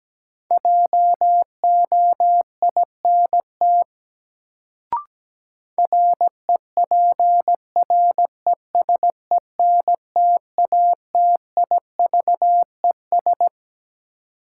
Morse code, heard as a keyed tone, about 17 wpm, 700 Hz. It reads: JOINT REPRESENTATIVES
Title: Sets of 5 Words Encore - New Word Drill - 17wpm